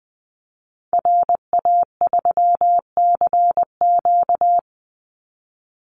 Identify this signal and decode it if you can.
Morse code, heard as a keyed tone, about 20 wpm, 700 Hz. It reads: RA3CQ
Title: Call Signs Repeated 3x - 20wpm